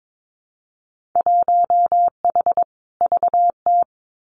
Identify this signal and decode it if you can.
Morse code, heard as a keyed tone, about 22 wpm, 700 Hz. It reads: 1H VT
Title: Winter Field Day Exchanges Encore 22wpm